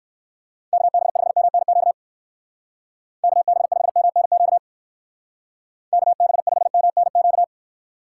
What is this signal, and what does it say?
Morse code, heard as a keyed tone, about 58 wpm, 700 Hz. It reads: KB5MNQ KB5MNQ KB5MNQ